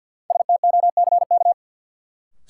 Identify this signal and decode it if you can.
Morse code, heard as a keyed tone, about 47 wpm, 700 Hz. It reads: STOCK
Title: Sets of 5 Words - New Word Drill - 50wpm